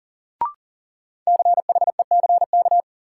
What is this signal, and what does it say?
Morse code, heard as a keyed tone, about 40 wpm, 700 Hz. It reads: CHECK
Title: Sentences from Top 1000 Words - Review Part 3 - New Word Drill - 40wpm